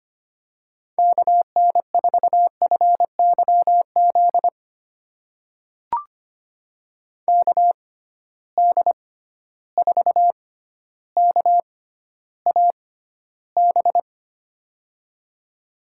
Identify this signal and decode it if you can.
Morse code, heard as a keyed tone, about 25 wpm, spaced as for 13 wpm, 700 Hz. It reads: KN4FYZ KD4KAB
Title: USA Call Signs - 6 Characters - 6x Character Spacing - 25wpm